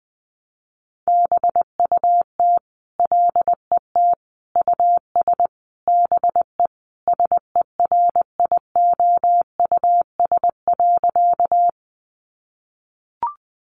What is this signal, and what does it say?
Morse code, heard as a keyed tone, about 20 wpm, 700 Hz. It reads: BUT LET US BE SERIOUS.
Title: Sentences from Top 1100 Words 20wpm